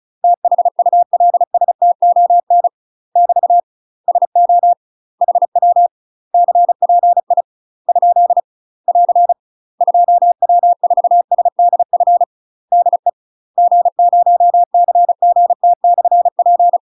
Morse code, heard as a keyed tone, about 34 wpm, 700 Hz. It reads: THULSTON <BT> SO HW CPI ? <AR> 2W4SDF DE G0CGT/P
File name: UK MASTER QSOs TRAINING / 35WPM / UK MASTER QSOs 0005